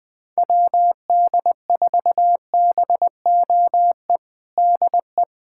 Morse code, heard as a keyed tone, about 20 wpm, 700 Hz. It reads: WD4BOE DE
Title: USA Callsign DE Callsign 20wpm